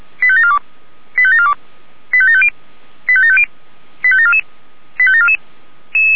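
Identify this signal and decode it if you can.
Five-tone selective call (ZVEI), heard as quick runs of five tones, tones of about 70 ms. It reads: 85742 85742 85750 85750 85744 85744
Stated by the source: ZVEI five-tone selective call received by SDR, from Feuerwehr Orlamünde.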